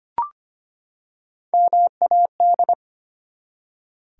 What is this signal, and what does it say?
Morse code, heard as a keyed tone, about 25 wpm, 700 Hz. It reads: MAD
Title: Three Letter Words - 25wpm